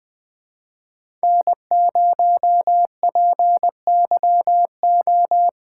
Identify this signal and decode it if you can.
Morse code, heard as a keyed tone, about 20 wpm, 700 Hz. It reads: N0PYO
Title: Speed Racing - USA 1x3 Call Signs 17wpm